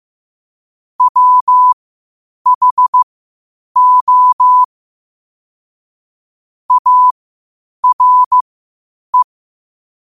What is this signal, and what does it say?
Morse code, heard as a keyed tone, about 15 wpm, spaced as for 8 wpm, 1000 Hz. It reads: WHO ARE
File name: qotdfiles / 8wpm